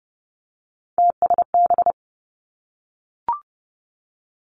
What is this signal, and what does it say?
Morse code, heard as a keyed tone, about 30 wpm, 700 Hz. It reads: TSB
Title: Three Challenging Letter-Number Combinations 30wpm